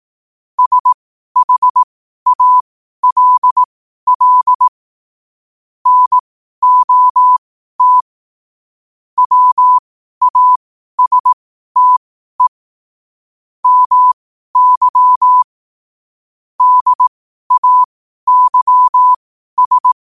Morse code, heard as a keyed tone, about 18 wpm, spaced as for 12 wpm, 1000 Hz. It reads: SHALL NOT WASTE MY DAYS
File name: qotdfiles / 12wpm